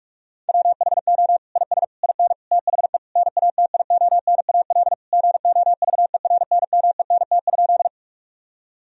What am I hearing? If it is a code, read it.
Morse code, heard as a keyed tone, about 41 wpm, 700 Hz. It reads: WHO IS IN THE NATIONAL GOVERNMENT?
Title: Sentences from Top 1000 Words - Review Part 4 - 45wpm